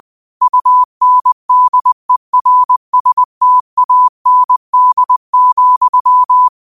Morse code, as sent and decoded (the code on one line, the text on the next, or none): ..- -. -.. .-. ... - .- -. -.. --..--
UNDERSTAND,